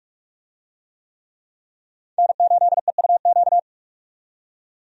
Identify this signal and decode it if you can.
Morse code, heard as a keyed tone, about 42 wpm, 700 Hz. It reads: N8EUQ
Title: USA Call Signs - 5 Characters - 45wpm